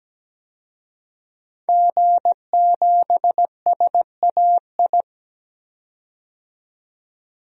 Morse code, heard as a keyed tone, about 17 wpm, 700 Hz. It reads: G7SAI
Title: DXCC-100 Call Signs - 5 Characters - 17wpm